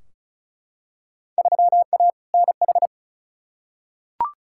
Morse code, heard as a keyed tone, about 36 wpm, 700 Hz. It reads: 3A NH
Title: Field Day Exchanges Encore - 35wpm